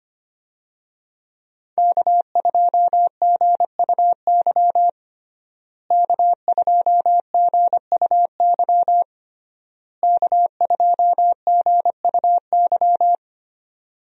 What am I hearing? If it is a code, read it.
Morse code, heard as a keyed tone, about 25 wpm, 700 Hz. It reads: K2GUY K2GUY K2GUY
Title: USA Call Signs - 5 Characters - Repeated 3x - 25wpm